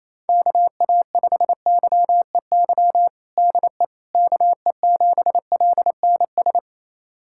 Morse code, heard as a keyed tone, about 28 wpm, 700 Hz. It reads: KA5YEY DE KE7LNS
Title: USA Callsign DE Callsign 28wpm